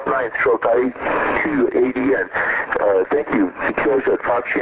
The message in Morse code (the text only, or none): none